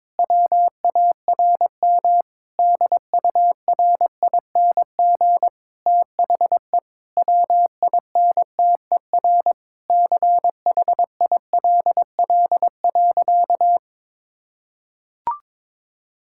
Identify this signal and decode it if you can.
Morse code, heard as a keyed tone, about 22 wpm, 700 Hz. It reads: WARM DURING THE WINTER CHILL.
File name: Difficult Sentences with Hints - 22wpm